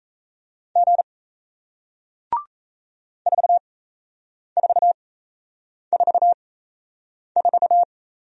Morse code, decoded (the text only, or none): G 4444